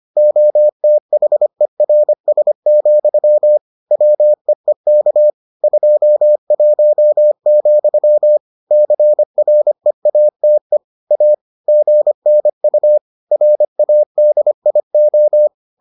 Morse code, heard as a keyed tone, about 25 wpm, 600 Hz. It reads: OTHERS, WEEK 21, CREATE A GNU RADIO